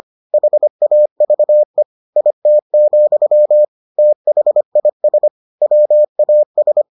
Morse code, heard as a keyed tone, about 25 wpm, 600 Hz. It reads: HAVE IT, THIS WAS